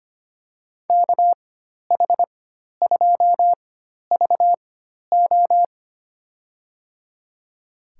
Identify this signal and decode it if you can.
Morse code, heard as a keyed tone, about 25 wpm, spaced as for 12 wpm, 700 Hz. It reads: KH2VO
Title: USA Call Signs - 5 Characters - 4x Character Spacing - 25wpm